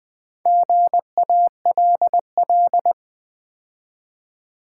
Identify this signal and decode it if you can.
Morse code, heard as a keyed tone, about 20 wpm, 700 Hz. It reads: GALL
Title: Build A Word - 2 Letters At A Time - 6-Letter Words - 20wpm